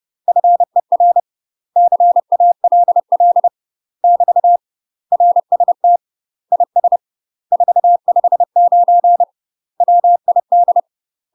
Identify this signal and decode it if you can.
Morse code, heard as a keyed tone, about 30 wpm, 700 Hz. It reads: FER CALL <BT> RST IS 459 WID
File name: UK MASTER QSOs TRAINING / 30WPM / UK MASTER QSOs 0028